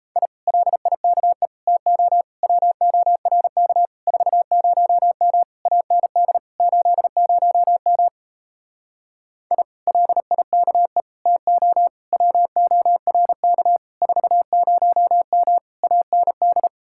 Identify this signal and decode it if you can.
Morse code, heard as a keyed tone, about 37 wpm, 700 Hz. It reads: I LIKE TO WORK 40M AND 80M I LIKE TO WORK 40M AND